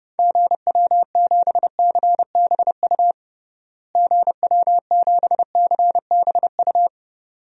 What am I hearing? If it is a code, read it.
Morse code, heard as a keyed tone, about 29 wpm, 700 Hz. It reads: GW7CBU GW7CBU
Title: DXCC-100 Call Signs - 6 Characters - Repeated 2x - 30wpm